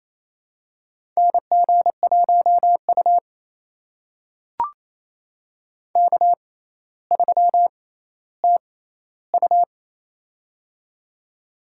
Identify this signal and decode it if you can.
Morse code, heard as a keyed tone, about 28 wpm, spaced as for 14 wpm, 700 Hz. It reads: NG1U K3TU
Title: USA Call Signs - 4 Characters - 6x Character Spacing - 28wpm